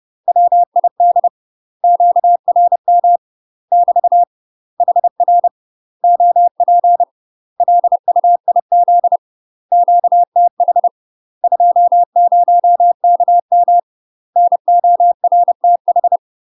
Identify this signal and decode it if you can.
Morse code, heard as a keyed tone, about 30 wpm, 700 Hz. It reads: WID QRM <BT> HR OP LUIZ QTH 20KM NORTH
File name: UK MASTER QSOs TRAINING/30WPM